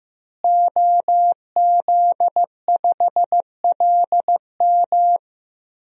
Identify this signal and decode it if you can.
Morse code, heard as a keyed tone, about 15 wpm, 700 Hz. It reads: OZ5LM
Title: DXCC-100 Call Signs - 5 Characters - Repeated 3x - 15wpm